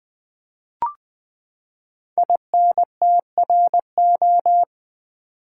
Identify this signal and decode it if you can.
Morse code, heard as a keyed tone, about 20 wpm, 700 Hz. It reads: INTRO
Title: Build A Word - 5 Letters At A Time - 12-Letter Words - 20wpm